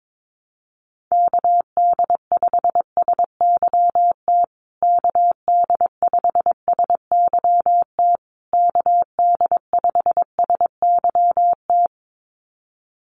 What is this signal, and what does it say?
Morse code, heard as a keyed tone, about 22 wpm, 700 Hz. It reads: KD5SYT KD5SYT KD5SYT
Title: USA Call Signs Repeated 3x 22wpm